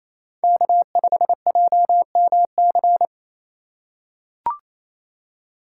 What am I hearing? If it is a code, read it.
Morse code, heard as a keyed tone, about 28 wpm, 700 Hz. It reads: K5JMC
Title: USA Call Signs - 5 Characters - 2x Character Spacing - 28wpm